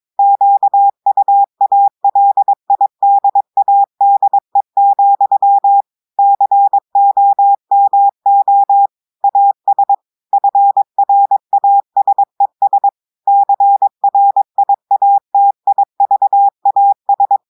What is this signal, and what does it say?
Morse code, heard as a keyed tone, about 22 wpm, 800 Hz. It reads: QUALIDADE, COMO AS FRASES CRIATIVAS
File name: audios / portugais / portugais texte en clair 2 vitesse 22 mots minutes